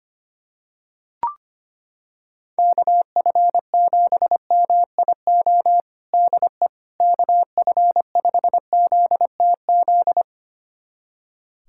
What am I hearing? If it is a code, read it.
Morse code, heard as a keyed tone, about 25 wpm, 700 Hz. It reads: KF7MIO DE KF5ZTZ
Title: USA Callsign DE Callsign 25wpm